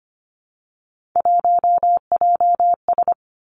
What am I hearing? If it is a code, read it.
Morse code, heard as a keyed tone, about 25 wpm, 700 Hz. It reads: 1JS
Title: Three Challenging Letter-Number Combinations 25wpm